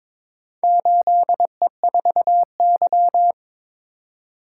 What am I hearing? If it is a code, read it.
Morse code, heard as a keyed tone, about 23 wpm, 700 Hz. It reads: 8E4Y
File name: practice / 22wpm / Four Letter-Number Combinations - 22wpm